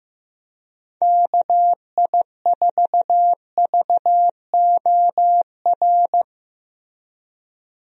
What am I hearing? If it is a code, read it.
Morse code, heard as a keyed tone, about 15 wpm, 700 Hz. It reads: KI4VOR
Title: USA Call Signs - 6 Characters - 15wpm